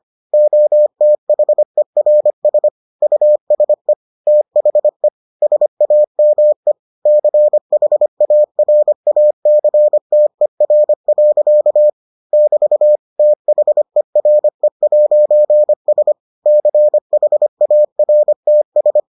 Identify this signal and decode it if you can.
Morse code, heard as a keyed tone, about 25 wpm, 600 Hz. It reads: OTHERS USE THE SAME CHARACTER. <BT> THERE'S CHARTS